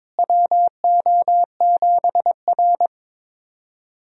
Morse code, heard as a keyed tone, about 22 wpm, 700 Hz. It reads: WO7R
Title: USA Call Signs - 4 Characters - Repeated 3x - 22wpm